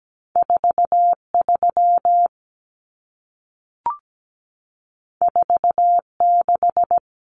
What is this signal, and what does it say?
Morse code, heard as a keyed tone, about 17 wpm, 700 Hz. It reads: 43 46
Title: Common Cut Numbers - 2-Digit 17wpm